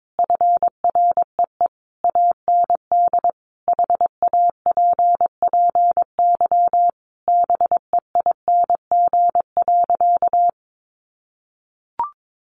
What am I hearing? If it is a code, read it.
Morse code, heard as a keyed tone, about 22 wpm, 700 Hz. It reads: FREE AND HAPPY BEING.
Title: Sentences from Top 1000 Words 22wpm